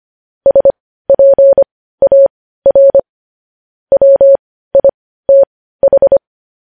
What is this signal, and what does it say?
Morse code, heard as a keyed tone, about 25 wpm, spaced as for 15 wpm, 550 Hz. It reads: SPAR WITH
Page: News Headlines in Morse Code at 15 WPM